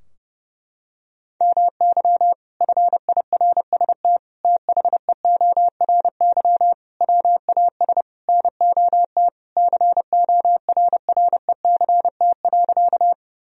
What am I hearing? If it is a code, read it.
Morse code, heard as a keyed tone, about 30 wpm, 700 Hz. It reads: MY FIRST THEORY WAS NOT CORRECT.
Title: Sentences from Top 1000 Words - Review Part 3 - 30wpm